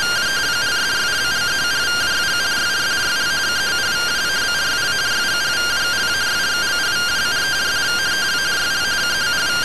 Two-tone teleprinter signal, about 170 Hz shift.